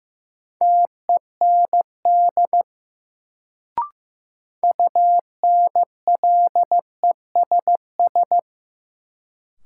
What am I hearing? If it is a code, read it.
Morse code, heard as a keyed tone, about 16 wpm, 700 Hz. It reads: TEND UNLESS
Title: Sentences from Top 1000 Words - Review Part 5 - New Word Drill - 15wpm